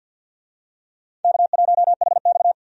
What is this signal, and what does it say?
Morse code, heard as a keyed tone, about 52 wpm, 700 Hz. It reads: K1HX
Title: USA Call Signs - 4 Characters - 50wpm